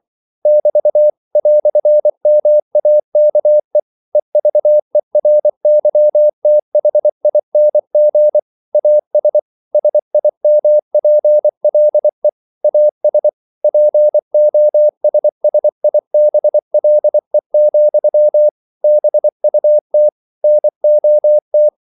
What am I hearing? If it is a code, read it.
Morse code, heard as a keyed tone, about 24 wpm, 600 Hz. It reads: <BT> "MAKE EVERYTHING AS SIMPLE AS POSSIBLE, BUT NOT
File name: CW / Practice / EnglishQuotations0224